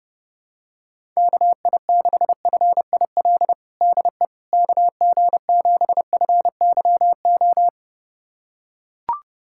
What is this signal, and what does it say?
Morse code, heard as a keyed tone, about 29 wpm, 700 Hz. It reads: KI6FIL DE KG7FYO
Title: USA Callsign DE Callsign 30wpm